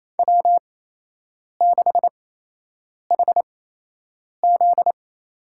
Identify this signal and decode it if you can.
Morse code, heard as a keyed tone, about 28 wpm, spaced as for 8 wpm, 700 Hz. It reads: W6HZ
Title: USA Call Signs - 8x Character Spacing - 28wpm